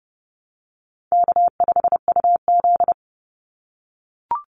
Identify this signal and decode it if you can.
Morse code, heard as a keyed tone, about 30 wpm, 700 Hz. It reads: K5UZ
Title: Easy Call Signs - 30wpm